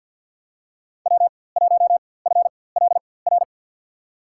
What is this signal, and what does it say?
Morse code, heard as a keyed tone, about 52 wpm, spaced as for 23 wpm, 700 Hz. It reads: W1FLR
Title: USA Call Signs - 5 Characters - 4x Character Spacing - 50wpm